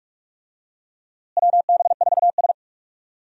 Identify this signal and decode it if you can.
Morse code, heard as a keyed tone, about 43 wpm, 700 Hz. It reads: WB4S
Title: USA Call Signs - 4 Characters - Repeated 2x - 45wpm